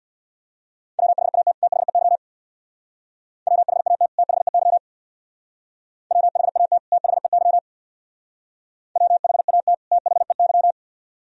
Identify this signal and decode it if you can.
Morse code, heard as a keyed tone, about 64 wpm, 700 Hz. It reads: WHAT THEY WHAT THEY WHAT THEY WHAT THEY